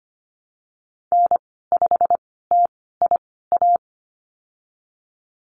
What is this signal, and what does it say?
Morse code, heard as a keyed tone, about 25 wpm, spaced as for 16 wpm, 700 Hz. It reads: N5TIA
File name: Lesson 15g - Number 5 - Call Signs 25wpm